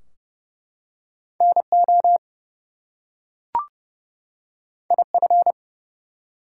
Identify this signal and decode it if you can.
Morse code, heard as a keyed tone, about 30 wpm, 700 Hz. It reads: NO IF
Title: Two Letter Words - 30wpm